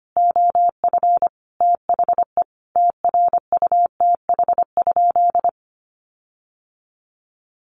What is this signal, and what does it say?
Morse code, heard as a keyed tone, about 25 wpm, 700 Hz. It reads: OF THE TRUTH?
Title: Sentences from Top 1400 Words 25wpm